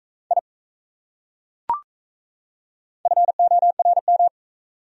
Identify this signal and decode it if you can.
Morse code, heard as a keyed tone, about 37 wpm, 700 Hz. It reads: I FORM